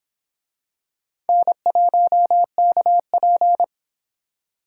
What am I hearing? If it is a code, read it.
Morse code, heard as a keyed tone, about 24 wpm, 700 Hz. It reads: N1KP